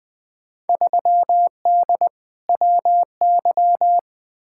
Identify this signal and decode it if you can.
Morse code, heard as a keyed tone, about 21 wpm, 700 Hz. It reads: 3D WY
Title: Field Day Exchanges Encore - 20wpm